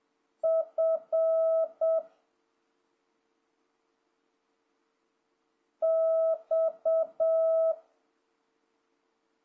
Morse code, heard as a keyed tone, about 7 wpm, 650 Hz. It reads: F X
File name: jh8cbh / 2026cw